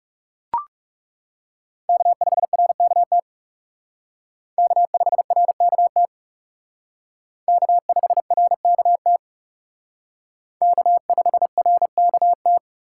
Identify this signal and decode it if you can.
Morse code, heard as a keyed tone, about 37 wpm, 700 Hz. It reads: K5RKT K5RKT K5RKT K5RKT
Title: Speed Racing - USA 1x3 Call Signs 30wpm